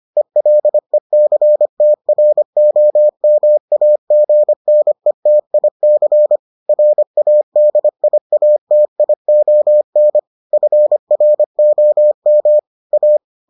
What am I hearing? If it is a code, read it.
Morse code, heard as a keyed tone, about 25 wpm, 600 Hz. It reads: ELECTROMAGNETIC RADIATION FROM A